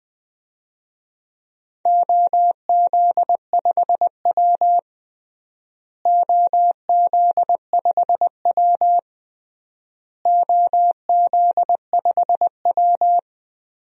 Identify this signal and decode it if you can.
Morse code, heard as a keyed tone, about 20 wpm, 690 Hz. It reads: OZ5W OZ5W OZ5W
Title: DXCC-100 Call Signs - 4 Characters - Repeated 3x - 20wpm